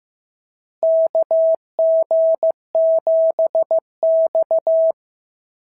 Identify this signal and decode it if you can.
Morse code, heard as a keyed tone, about 14 wpm, 650 Hz. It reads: KG7X
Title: SOTA QSO Practice 15wpm